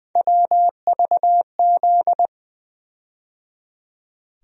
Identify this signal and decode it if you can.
Morse code, heard as a keyed tone, about 20 wpm, 700 Hz. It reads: WVZ
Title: USA Call Sign Postfixes - 3 Letters 20wpm